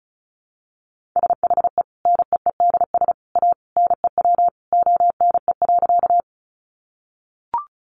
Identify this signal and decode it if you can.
Morse code, heard as a keyed tone, about 34 wpm, 700 Hz. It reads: SHE NEEDS A NEW ONE.
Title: Additional Easy Phrases 35wpm